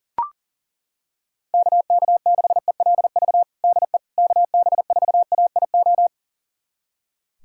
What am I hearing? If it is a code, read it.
Morse code, heard as a keyed tone, about 39 wpm, 700 Hz. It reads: KK6ELV DE KB4AIO
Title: USA Callsign DE Callsign - 6 Characters - 40wpm